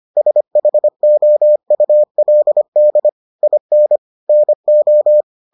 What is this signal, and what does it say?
Morse code, heard as a keyed tone, about 25 wpm, 600 Hz. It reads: SHOULD IN NO